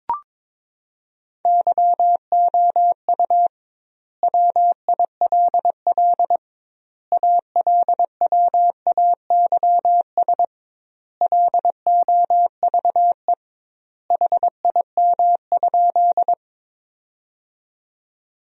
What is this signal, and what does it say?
Morse code, heard as a keyed tone, about 22 wpm, 700 Hz. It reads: YOU WILL ALWAYS LOVE HIM?
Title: 2x Word-Spacing - Sentences from Top 400 Words - 22wpm